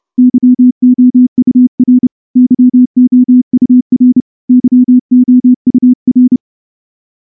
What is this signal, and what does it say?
Morse code, heard as a keyed tone, about 28 wpm, 250 Hz. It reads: YOUR YOUR YOUR
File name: morse-training-30WPM